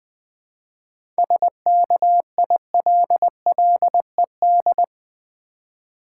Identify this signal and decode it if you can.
Morse code, heard as a keyed tone, about 20 wpm, 700 Hz. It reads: SKILLED
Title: 7 Letter Words - First 3 Letters Given - 20wpm